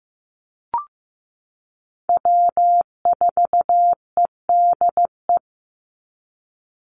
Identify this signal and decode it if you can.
Morse code, heard as a keyed tone, about 15 wpm, 700 Hz. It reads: W4EDE
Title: Call Signs Repeated 2x - 15wpm